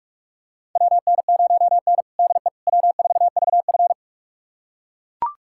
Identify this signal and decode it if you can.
Morse code, heard as a keyed tone, about 43 wpm, 700 Hz. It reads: WN0N DE W4VF
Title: USA Callsign DE Callsign - 4 Characters - 45wpm